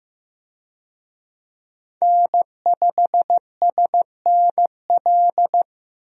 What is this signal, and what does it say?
Morse code, heard as a keyed tone, about 15 wpm, 700 Hz. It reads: N5SNL
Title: USA Call Signs - 5 Characters - Repeated 2x - 15wpm